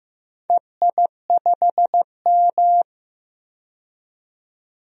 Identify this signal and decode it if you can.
Morse code, heard as a keyed tone, about 15 wpm, 700 Hz. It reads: EI5M
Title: DXCC-100 Call Signs - 4 Characters - 15wpm